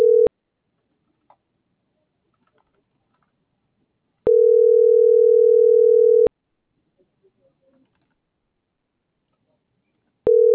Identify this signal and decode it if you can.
Ringback tone (North America), 440 + 480 Hz.